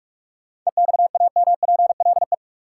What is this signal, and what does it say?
Morse code, heard as a keyed tone, about 44 wpm, 700 Hz. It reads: EXAMPLE
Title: Sets of 2 Words - New Word Drill - 45wpm